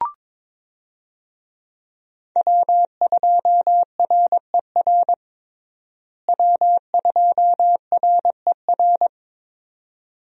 Morse code, heard as a keyed tone, about 22 wpm, 700 Hz. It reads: W2RER W2RER
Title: USA Call Signs - 5 Characters - Repeated 2x - 22wpm